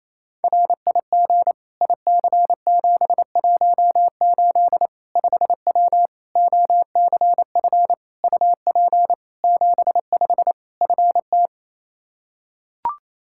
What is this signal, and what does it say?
Morse code, heard as a keyed tone, about 27 wpm, 700 Hz. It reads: RIG IC718 5W OCF UP 75 FT
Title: K1USN SST QSO Practice 28wpm